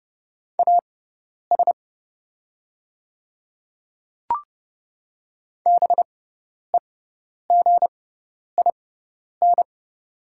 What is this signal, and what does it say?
Morse code, heard as a keyed tone, about 32 wpm, spaced as for 10 wpm, 690 Hz. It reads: AS BEGIN
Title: 6x Character Spacing - Top 500 Words - 30wpm